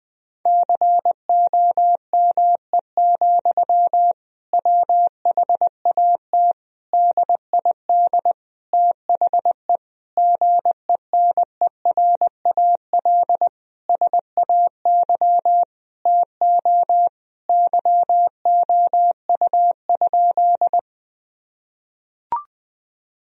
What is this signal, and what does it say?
Morse code, heard as a keyed tone, about 20 wpm, 700 Hz. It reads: COME, WHAT DID THE GENERAL SAY TO YOU?
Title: Sentences from Top 1000 Words - Review Part 3 - 20wpm